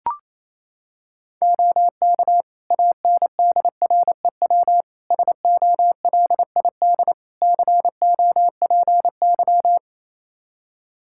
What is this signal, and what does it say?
Morse code, heard as a keyed tone, about 28 wpm, 700 Hz. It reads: OK ANDREW SOLID COPY
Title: Practice QSO - Exchange 3 - More Info - 28wpm